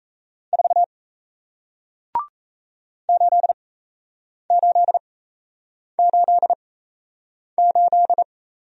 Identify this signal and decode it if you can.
Morse code, heard as a keyed tone, about 38 wpm, spaced as for 8 wpm, 700 Hz. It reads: 4 8888